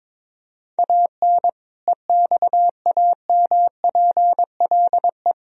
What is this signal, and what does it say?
Morse code, heard as a keyed tone, about 22 wpm, 700 Hz. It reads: AN EXAMPLE
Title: Sentences from Top 1000 Words - Review Part 1 - 22wpm